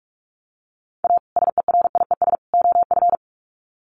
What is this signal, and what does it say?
Morse code, heard as a keyed tone, about 44 wpm, 700 Hz. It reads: A SERIES OF
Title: Sets of 3 Words 45wpm